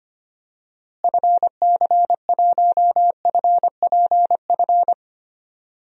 Morse code, heard as a keyed tone, about 25 wpm, 700 Hz. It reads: FC1FPF
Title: DXCC-100 Call Signs - 6 Characters - Repeated 3x - 25wpm